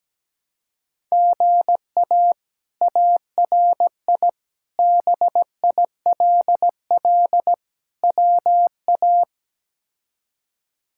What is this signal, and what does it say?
Morse code, heard as a keyed tone, about 17 wpm, 700 Hz. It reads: GA ARI BILL WA
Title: K1USN SST QSO Practice - 17wpm